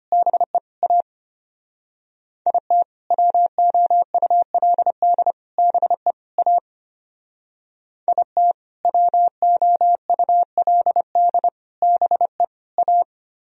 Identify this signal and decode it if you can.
Morse code, heard as a keyed tone, about 31 wpm, 700 Hz. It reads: BE A IT WOULD BE A IT WOULD BE A
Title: Speed-Racing - Sets of 4 Words 25wpm